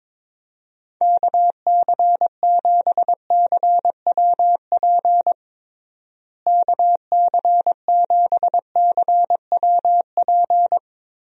Morse code, heard as a keyed tone, about 22 wpm, 700 Hz. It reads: KC7CWP KC7CWP
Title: USA Call Signs - 6 Characters - Repeated 3x - 22wpm